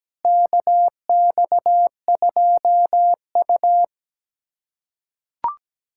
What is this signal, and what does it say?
Morse code, heard as a keyed tone, about 17 wpm, 700 Hz. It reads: KX2U
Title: USA Callsign DE Callsign - 4 Characters - 17wpm